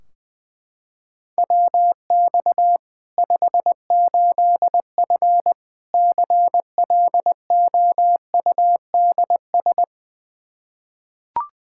Morse code, heard as a keyed tone, about 20 wpm, 700 Hz. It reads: WX 58F CLOUDS
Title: Practice QSO - Exchange 2 - Info - 20wpm